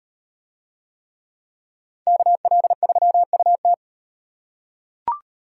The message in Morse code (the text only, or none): KL3UT